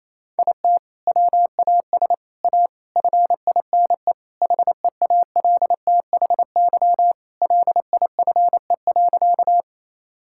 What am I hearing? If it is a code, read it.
Morse code, heard as a keyed tone, about 28 wpm, 700 Hz. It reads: IT WAS A FINE HEALTHY LIFE.
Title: Sentences from Top 1000 Words - Review Part 5 - 28wpm